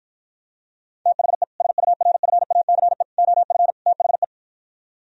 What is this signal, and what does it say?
Morse code, heard as a keyed tone, about 52 wpm, 700 Hz. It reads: THE SURFACE OF THE